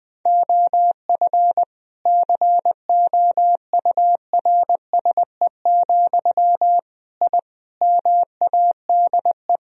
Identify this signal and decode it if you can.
Morse code, heard as a keyed tone, about 20 wpm, 700 Hz. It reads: OF COURSE, I MADE
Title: Sentences from Top 1000 Words - Review Part 1 - 20wpm